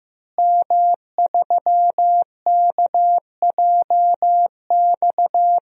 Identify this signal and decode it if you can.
Morse code, heard as a keyed tone, about 15 wpm, 700 Hz. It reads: M3KJX
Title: DXCC-100 Call Signs - 5 Characters - Repeated 3x - 15wpm